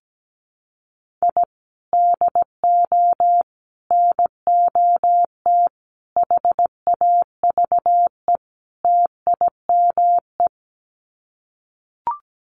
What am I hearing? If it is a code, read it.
Morse code, heard as a keyed tone, about 17 wpm, 700 Hz. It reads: I DO NOT HAVE TIME
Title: Sets of 5 Words Encore 17wpm